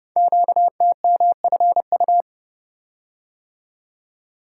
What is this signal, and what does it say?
Morse code, heard as a keyed tone, about 30 wpm, 690 Hz. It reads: QTMFU
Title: Code Groups - 5 Letters - 30wpm